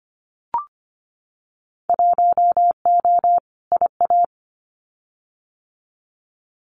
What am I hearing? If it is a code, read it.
Morse code, heard as a keyed tone, about 25 wpm, 700 Hz. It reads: 1O IA
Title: Winter Field Day Exchanges 25wpm